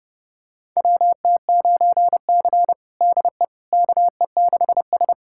Morse code, heard as a keyed tone, about 30 wpm, 700 Hz. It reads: WT9C DE KE6S
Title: USA Callsign DE Callsign - 4 Characters - 30wpm